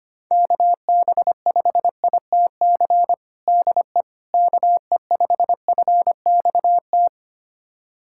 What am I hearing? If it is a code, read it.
Morse code, heard as a keyed tone, about 25 wpm, 700 Hz. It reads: KB5ITC DE KE5FXT
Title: USA Callsign DE Callsign - 6 Characters - 25wpm